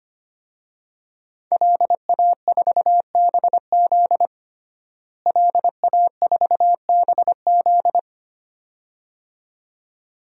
Morse code, heard as a keyed tone, about 25 wpm, 700 Hz. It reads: LA4BZ LA4BZ
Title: DXCC-100 Call Signs - 5 Characters - Repeated 2x - 25wpm